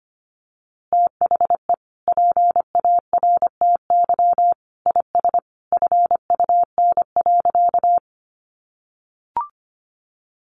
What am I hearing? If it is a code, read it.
Morse code, heard as a keyed tone, about 25 wpm, 700 Hz. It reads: THE PARTY IS FUN.
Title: Easy Phrases 25wpm